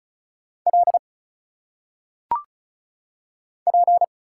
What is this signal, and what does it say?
Morse code, heard as a keyed tone, about 34 wpm, 700 Hz. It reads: L P
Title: Single Letters - Phonetic Pronunciation 35wpm